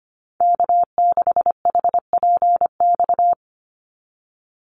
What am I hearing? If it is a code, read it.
Morse code, heard as a keyed tone, about 25 wpm, 700 Hz. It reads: K6HPX
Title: NA SOTA Chaser Callsigns 25wpm